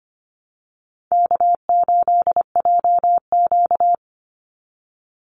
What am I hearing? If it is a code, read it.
Morse code, heard as a keyed tone, about 25 wpm, 700 Hz. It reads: K8JQ
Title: Easy Call Signs - 25wpm